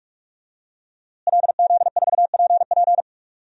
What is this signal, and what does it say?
Morse code, heard as a keyed tone, about 46 wpm, 700 Hz. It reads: LZ4PP
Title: DXCC-100 Call Signs - 5 Characters - Repeated 3x - 45wpm